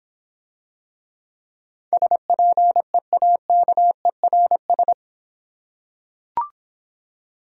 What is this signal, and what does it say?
Morse code, heard as a keyed tone, about 26 wpm, 700 Hz. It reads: SPEAKERS